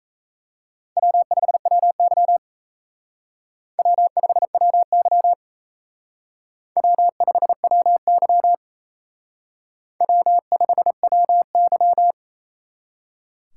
Morse code, heard as a keyed tone, about 36 wpm, 700 Hz. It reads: W5WY W5WY W5WY W5WY